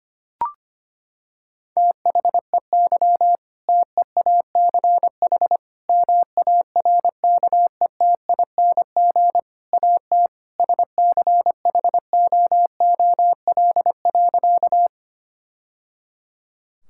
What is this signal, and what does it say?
Morse code, heard as a keyed tone, about 25 wpm, 700 Hz. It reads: THEY TEACH MARKETING AT SCHOOL.
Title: Sentences from Top 1000 Words - Review Part 5 - 25wpm